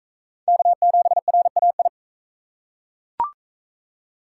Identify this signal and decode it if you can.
Morse code, heard as a keyed tone, about 39 wpm, 700 Hz. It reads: K7RAI